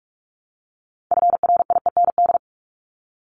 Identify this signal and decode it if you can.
Morse code, heard as a keyed tone, about 44 wpm, 700 Hz. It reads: FRIEND
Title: Top 500 Words 45wpm